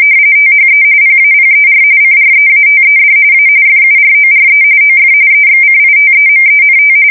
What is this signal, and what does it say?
RTTY, about 45.45 baud. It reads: THE PAPER CREATED A FURORE OF EXCITEMENT